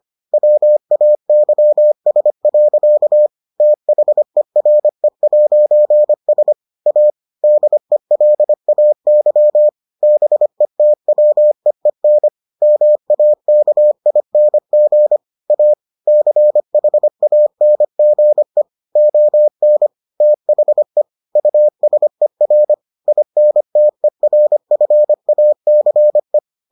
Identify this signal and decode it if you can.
Morse code, heard as a keyed tone, about 25 wpm, 600 Hz. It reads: WAYS. THERE'S A DELAY BETWEEN MAKING A CHANGE ON THE USER INTERFACE